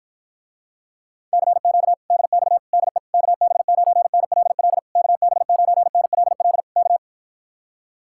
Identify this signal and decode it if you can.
Morse code, heard as a keyed tone, about 52 wpm, 700 Hz. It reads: CQ DX DE KB9NLL KB9NLL K